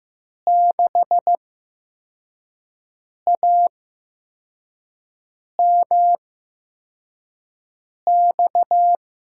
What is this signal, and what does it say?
Morse code, heard as a keyed tone, about 15 wpm, spaced as for 4 wpm, 700 Hz. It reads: 6AMX
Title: USA Call Signs - 6 Characters - 8x Character Spacing - 15wpm